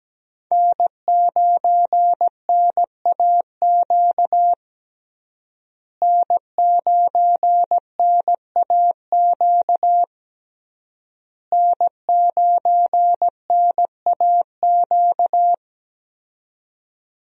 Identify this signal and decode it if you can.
Morse code, heard as a keyed tone, about 17 wpm, 700 Hz. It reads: N9NAQ N9NAQ N9NAQ
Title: USA Call Signs - 5 Characters - Repeated 3x - 17wpm